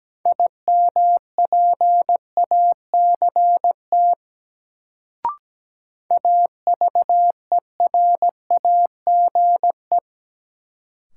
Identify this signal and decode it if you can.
Morse code, heard as a keyed tone, about 17 wpm, 700 Hz. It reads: IMPACT AVERAGE
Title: Sentences from Top 1000 Words - Review Part 6 - New Word Drill - 17wpm